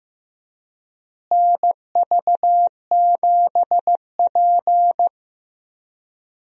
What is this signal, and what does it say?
Morse code, heard as a keyed tone, about 15 wpm, 700 Hz. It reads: NV7P
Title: USA Call Signs - 4 Characters - Repeated 3x - 15wpm